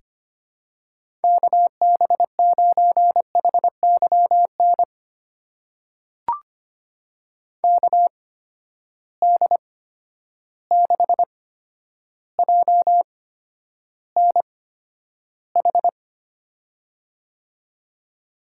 Morse code, heard as a keyed tone, about 25 wpm, spaced as for 11 wpm, 700 Hz. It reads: KB9HYN KD6JNH
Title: USA Call Signs - 6 Characters - 8x Character Spacing - 25wpm